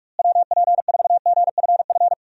Morse code, heard as a keyed tone, about 46 wpm, 700 Hz. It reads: WP4GFF
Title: USA Call Signs - 6 Characters - Repeated 3x - 45wpm